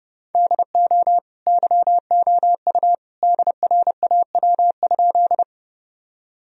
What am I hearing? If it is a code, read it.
Morse code, heard as a keyed tone, about 30 wpm, 700 Hz. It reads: DO YOU DRAW?
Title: Sentences from Top 1000 Words - Review Part 1 - 30wpm